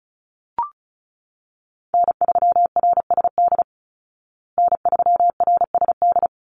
Morse code, heard as a keyed tone, about 34 wpm, 700 Hz. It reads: N3RSD N3RSD
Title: Call Signs Repeated 3x - 35wpm